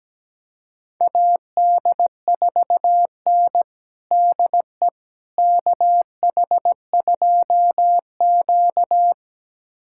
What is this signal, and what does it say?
Morse code, heard as a keyed tone, about 17 wpm, 700 Hz. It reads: AD4N DE KH2Q
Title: USA Callsign DE Callsign - 4 Characters - 17wpm